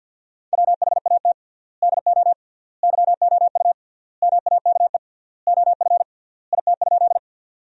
Morse code, heard as a keyed tone, about 50 wpm, 700 Hz. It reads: WHAT DO YOU MAKE OF IT?
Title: 3x Word-Spacing - Sentences from Top 100 Words - 50wpm